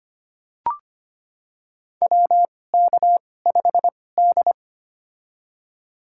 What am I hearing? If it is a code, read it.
Morse code, heard as a keyed tone, about 25 wpm, spaced as for 18 wpm, 700 Hz. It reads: WK5D
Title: USA Call Signs - 4 Characters - 2x Character Spacing - 25wpm